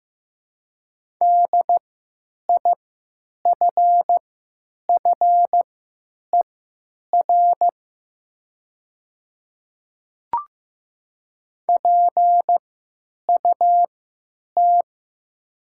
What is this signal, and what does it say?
Morse code, heard as a keyed tone, about 15 wpm, spaced as for 9 wpm, 700 Hz. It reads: DIFFER PUT
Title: 3x Character Spacing - Top 200 Words - 15wpm